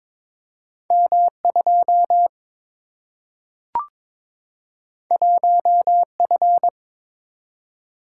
Morse code, heard as a keyed tone, about 22 wpm, 700 Hz. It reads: M 2 1 F
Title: Single-Letter Single-Number 22wpm